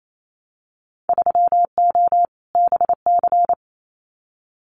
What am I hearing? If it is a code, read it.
Morse code, heard as a keyed tone, about 28 wpm, 690 Hz. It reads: 3O BC
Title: Winter Field Day Exchanges Encore 28wpm